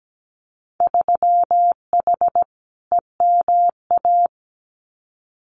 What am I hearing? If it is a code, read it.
Morse code, heard as a keyed tone, about 17 wpm, 700 Hz. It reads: 3H EMA
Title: Winter Field Day Exchanges Encore 17wpm